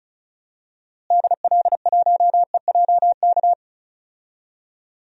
Morse code, heard as a keyed tone, about 36 wpm, 700 Hz. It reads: DL1EJK
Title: DXCC-100 Call Signs - 6 Characters - Repeated 3x - 35wpm